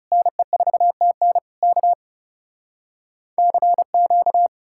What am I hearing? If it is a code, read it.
Morse code, heard as a keyed tone, about 40 wpm, 700 Hz. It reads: NE4TN K CQ